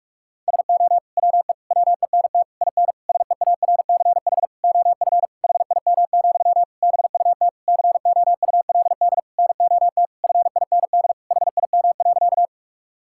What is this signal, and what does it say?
Morse code, heard as a keyed tone, about 42 wpm, 700 Hz. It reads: SO WE WENT IN SEARCH OF HIM, BUT COULD NOT FIND HIM.
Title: Sentences from Top 1000 Words - Review Part 5 - 45wpm